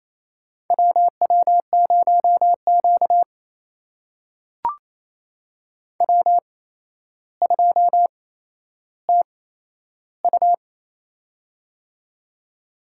Morse code, heard as a keyed tone, about 28 wpm, spaced as for 12 wpm, 700 Hz. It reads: WW0Q W2TU
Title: USA Call Signs - 4 Characters - 8x Character Spacing - 28wpm